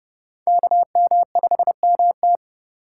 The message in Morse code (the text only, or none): KM5MT